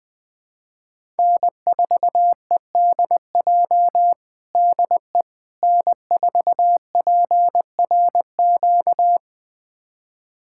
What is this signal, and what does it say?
Morse code, heard as a keyed tone, about 20 wpm, 700 Hz. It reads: N4EDJ DE N4PRQ
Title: USA Callsign DE Callsign - 5 Characters - 20wpm